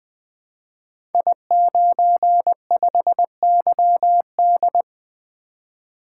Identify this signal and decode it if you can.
Morse code, heard as a keyed tone, about 20 wpm, 700 Hz. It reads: I95YD
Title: Code Groups - 5 Letter:Numbers - 20wpm